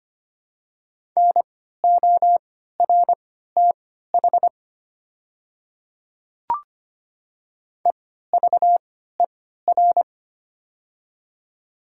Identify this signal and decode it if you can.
Morse code, heard as a keyed tone, about 25 wpm, spaced as for 14 wpm, 700 Hz. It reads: NORTH EVER
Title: 3x Character Spacing - Top 500 Words - 25wpm